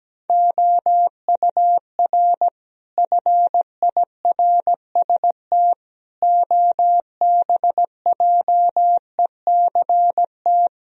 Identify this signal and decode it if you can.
Morse code, heard as a keyed tone, about 17 wpm, 700 Hz. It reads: OUR FIRST OBJECT
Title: Sentences from Top 1000 Words - Review Part 3 - 17wpm